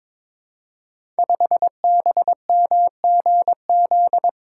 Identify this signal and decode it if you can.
Morse code, heard as a keyed tone, about 23 wpm, 700 Hz. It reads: 5BMGZ
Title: Code Groups - 5 Letter:Numbers - 22wpm